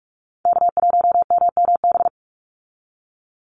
Morse code, heard as a keyed tone, about 44 wpm, 700 Hz. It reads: K1MMB
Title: Call Signs - 45wpm